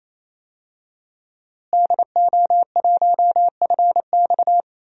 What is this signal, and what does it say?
Morse code, heard as a keyed tone, about 28 wpm, 700 Hz. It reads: DO1FX
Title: DXCC-100 Call Signs - 5 Characters - Repeated 3x - 28wpm